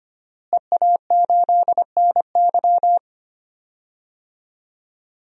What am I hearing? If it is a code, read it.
Morse code, heard as a keyed tone, about 25 wpm, 700 Hz. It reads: EA8NY
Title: DXCC-100 Call Signs - 5 Characters - Repeated 2x - 25wpm